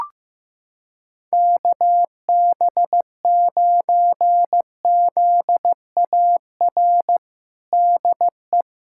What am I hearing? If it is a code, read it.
Morse code, heard as a keyed tone, about 15 wpm, 700 Hz. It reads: KB9ZAR DE
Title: USA Callsign DE Callsign - 6 Characters - 15wpm